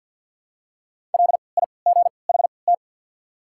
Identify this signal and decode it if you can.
Morse code, heard as a keyed tone, about 52 wpm, spaced as for 29 wpm, 690 Hz. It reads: LIGHT
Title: 3x Character Spacing - Top 200 Words - 50wpm